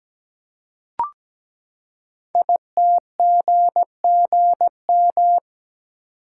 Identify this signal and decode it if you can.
Morse code, heard as a keyed tone, about 17 wpm, 700 Hz. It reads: ITGGM
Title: Code Groups - 5 Letters - 17wpm